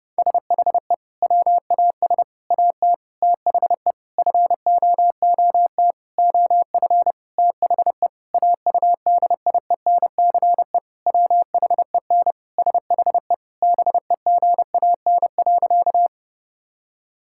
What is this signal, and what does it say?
Morse code, heard as a keyed tone, about 32 wpm, 700 Hz. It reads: SHE WAS AT THE FOOT OF THE AUDIENCE WHEN SHE BEGAN.
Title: Sentences from Top 1000 Words - Review Part 5 - 30wpm